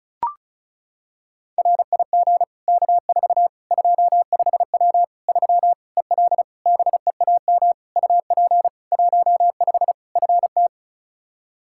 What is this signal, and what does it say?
Morse code, heard as a keyed tone, about 36 wpm, 700 Hz. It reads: RIG K4 25W 3 EL BEAM UP 15 FT
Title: Practice QSO - Exchange 2 - Info - 35wpm